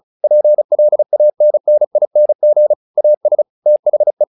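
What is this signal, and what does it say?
Morse code, heard as a keyed tone, about 35 wpm, 600 Hz. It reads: PLANNING AS THE
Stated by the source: Morse practice text "Tower Trouble" sent at 35 WPM, from the SKCC - Straight Key Century Club learning center audio files.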